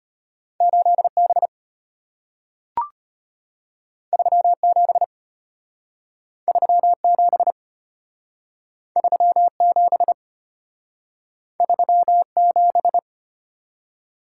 Morse code, heard as a keyed tone, about 33 wpm, 700 Hz. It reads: 8 B 3 7 3 7 3 7 3 7